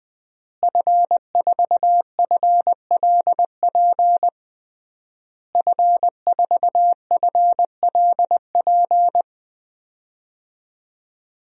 Morse code, heard as a keyed tone, about 20 wpm, 700 Hz. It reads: F4FLP F4FLP
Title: DXCC-100 Call Signs - 5 Characters - Repeated 3x - 20wpm